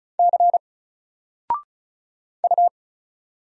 Morse code, heard as a keyed tone, about 35 wpm, 700 Hz. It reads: C U
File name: Single-Letter-Number-35wpm